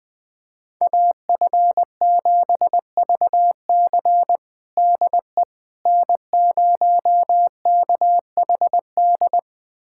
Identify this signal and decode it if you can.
Morse code, heard as a keyed tone, about 20 wpm, 700 Hz. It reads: AF7VC DE N0KHD
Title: USA Callsign DE Callsign - 5 Characters - 20wpm